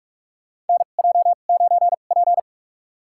Morse code, heard as a keyed tone, about 46 wpm, spaced as for 33 wpm, 700 Hz. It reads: NJ9P
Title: USA Call Signs - 4 Characters - 2x Character Spacing - 45wpm